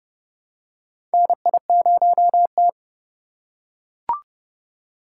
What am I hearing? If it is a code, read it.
Morse code, heard as a keyed tone, about 32 wpm, 700 Hz. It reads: NI0T
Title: USA Call Signs - 2x Character Spacing - 30wpm